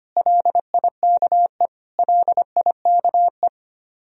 Morse code, heard as a keyed tone, about 25 wpm, 700 Hz. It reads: LIKE LIKE
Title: Top 100 Words Repeated 25wpm